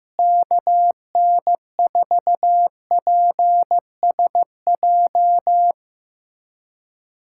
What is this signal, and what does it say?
Morse code, heard as a keyed tone, about 15 wpm, 700 Hz. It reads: KN4PSJ
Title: USA Callsign DE Callsign 15wpm